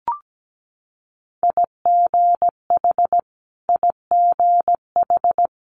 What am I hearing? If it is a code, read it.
Morse code, heard as a keyed tone, about 17 wpm, 700 Hz. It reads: IGH IGH
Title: Multi-letter Phonemes Repeated 3x 17wpm